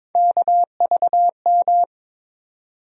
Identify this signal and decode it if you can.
Morse code, heard as a keyed tone, about 22 wpm, 700 Hz. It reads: KVM
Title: USA Call Sign Postfixes - 3 Letters 22wpm